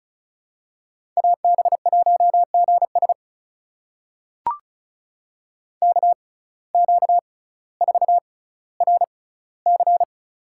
Morse code, heard as a keyed tone, about 34 wpm, spaced as for 18 wpm, 700 Hz. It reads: AB1GS KQ4RC
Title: USA Call Signs - 5 Characters - 6x Character Spacing - 35wpm